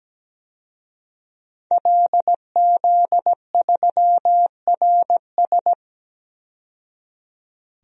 Morse code, heard as a keyed tone, about 17 wpm, 700 Hz. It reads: LZ3RS
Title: DXCC-100 Call Signs - 5 Characters - 17wpm